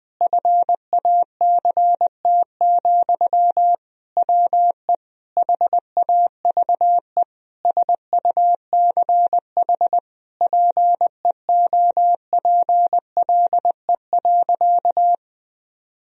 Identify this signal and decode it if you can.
Morse code, heard as a keyed tone, about 20 wpm, 700 Hz. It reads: FACT, WE HAVE SUCH PEOPLE.
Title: Sentences from Top 1000 Words - Review Part 3 - 20wpm